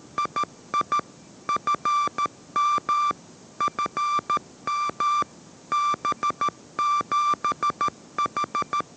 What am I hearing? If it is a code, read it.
Morse code, heard as a keyed tone, about 13 wpm, 1180 Hz. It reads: II FM FM B7H
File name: 14wpm